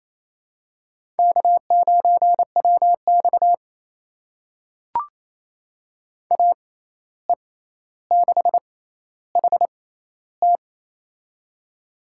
Morse code, heard as a keyed tone, about 28 wpm, spaced as for 13 wpm, 700 Hz. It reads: K9WX AE6HT
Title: USA Call Signs - 6x Character Spacing - 28wpm